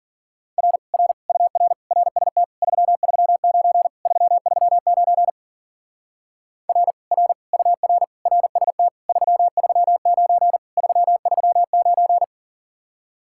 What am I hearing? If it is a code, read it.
Morse code, heard as a keyed tone, about 40 wpm, 700 Hz. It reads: R R UR RST 339 339 R R UR RST 339 339
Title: Speed-Racing Common QSO Phrases - 40wpm